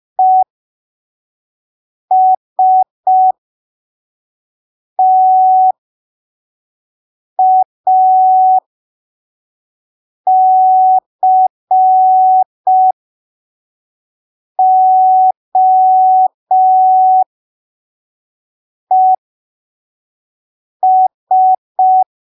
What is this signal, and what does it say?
Morse code, heard as a keyed tone, about 5 wpm, spaced as for 3 wpm, 750 Hz. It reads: ESTACOES
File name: arquivos / texto / texto 03 5ppm